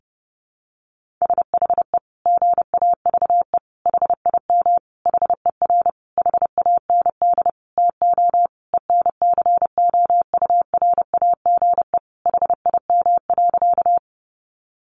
Morse code, heard as a keyed tone, about 30 wpm, 700 Hz. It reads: SHE GAVE HIM HER HAND TO ENCOURAGE HIM.
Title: Sentences from Top 1400 Words 30wpm